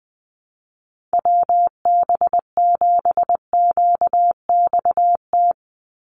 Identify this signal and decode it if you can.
Morse code, heard as a keyed tone, about 20 wpm, 700 Hz. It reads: WB7QXT
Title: USA Call Signs 20wpm